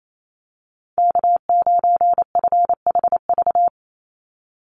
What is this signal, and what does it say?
Morse code, heard as a keyed tone, about 28 wpm, 700 Hz. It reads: K9FHV
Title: USA Call Signs 28wpm